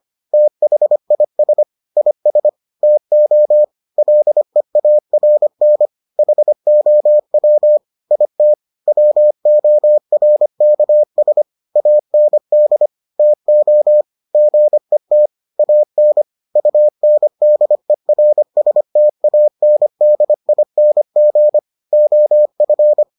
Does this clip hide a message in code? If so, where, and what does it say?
Yes: Morse code from 0.3 to 23.0 s, heard as a keyed tone, about 25 wpm, 600 Hz. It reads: THIS IS TO LEARN HOW IT WORKS AND TO GET AN UNDERSTANDING OF